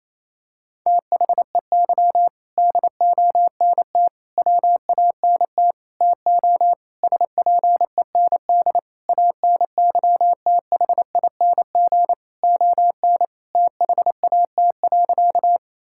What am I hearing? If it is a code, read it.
Morse code, heard as a keyed tone, about 28 wpm, 700 Hz. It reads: THEY DONT WANT TO SPEND ANYTHING ON THAT.
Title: Sentences from Top 1000 Words - Review Part 5 - 28wpm